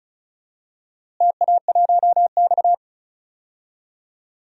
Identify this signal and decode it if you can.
Morse code, heard as a keyed tone, about 36 wpm, 700 Hz. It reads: TA1X
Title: DXCC-100 Call Signs - 4 Characters - Repeated 3x - 35wpm